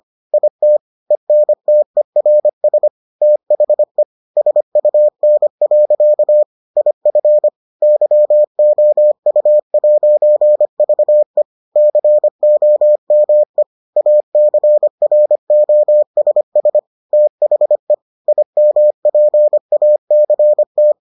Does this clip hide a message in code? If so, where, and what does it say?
Yes: Morse code from 0.3 to 20.9 s, heard as a keyed tone, about 25 wpm, 600 Hz. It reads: IT ENTERS THE SUN. IF YOU'VE COME ACROSS THE IMPACT